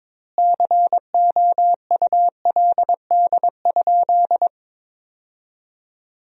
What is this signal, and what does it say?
Morse code, heard as a keyed tone, about 22 wpm, 700 Hz. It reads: COULD?
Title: 2x Word-Spacing - Sentences from Top 400 Words - 22wpm